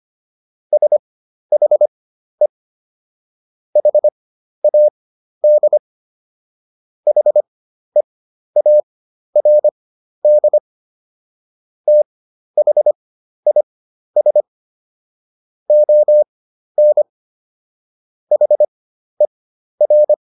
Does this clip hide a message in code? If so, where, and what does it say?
Yes: Morse code from 0.7 to 20.1 s, heard as a keyed tone, about 25 wpm, spaced as for 12 wpm, 600 Hz. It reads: SHE HAD HEARD THIS ON HER